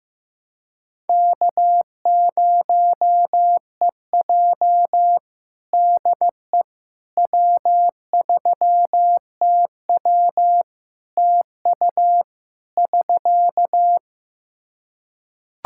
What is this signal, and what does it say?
Morse code, heard as a keyed tone, about 16 wpm, 700 Hz. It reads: K0EJ DE W3TW TU <SK>
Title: Practice QSO - Exchange 4 - Ending - 15wpm